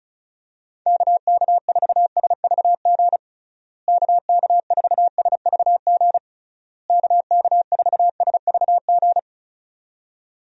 Morse code, heard as a keyed tone, about 34 wpm, 700 Hz. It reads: KK4SVG KK4SVG KK4SVG
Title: USA Call Signs - 6 Characters - Repeated 3x - 35wpm